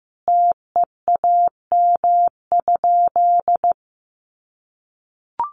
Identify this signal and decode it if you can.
Morse code, heard as a keyed tone, about 15 wpm, 700 Hz. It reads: TEAM?
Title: Sentences from Top 1500 Words 15wpm